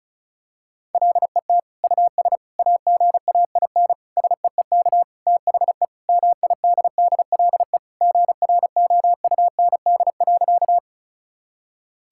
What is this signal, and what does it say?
Morse code, heard as a keyed tone, about 36 wpm, 700 Hz. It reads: LET US AGAIN SEEK THE MIDDLE GROUND.
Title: Sentences from Top 1600 Words 35wpm